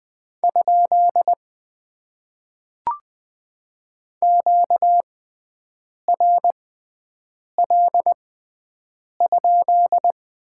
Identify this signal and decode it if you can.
Morse code, heard as a keyed tone, about 20 wpm, spaced as for 7 wpm, 700 Hz. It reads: ? QRL?
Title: Top 90 QSO Elements - 6x Character Spacing - 20wpm